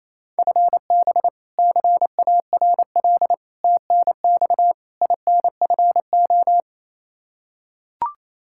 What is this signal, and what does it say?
Morse code, heard as a keyed tone, about 28 wpm, 700 Hz. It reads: FB CARL TNX INFO
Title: Practice QSO - Exchange 3 - More Info - 28wpm